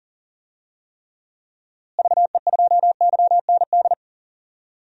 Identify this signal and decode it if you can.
Morse code, heard as a keyed tone, about 37 wpm, 700 Hz. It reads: VE2YND
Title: DXCC-100 Call Signs - 6 Characters - Repeated 3x - 40wpm